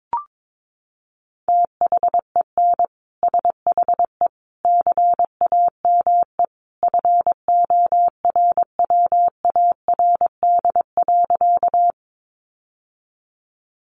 Morse code, heard as a keyed tone, about 22 wpm, 700 Hz. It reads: THEN SHE CAME FORWARD.
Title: Sentences from Top 1300 Words 22wpm